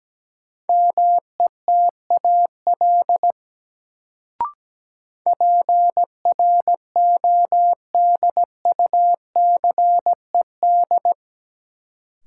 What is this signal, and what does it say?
Morse code, heard as a keyed tone, about 17 wpm, 700 Hz. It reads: METAL PRODUCED
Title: Sentences from Top 1000 Words - Review Part 6 - New Word Drill - 17wpm